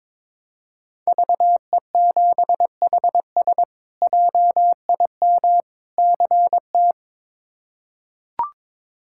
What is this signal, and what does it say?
Morse code, heard as a keyed tone, about 22 wpm, 700 Hz. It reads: VE7HS JIM CT
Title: K1USN SST QSO Practice - 22wpm